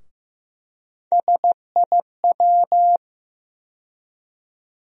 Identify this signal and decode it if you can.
Morse code, heard as a keyed tone, about 15 wpm, 700 Hz. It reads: SIW
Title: USA Call Sign Postfixes - 3 Letters 15wpm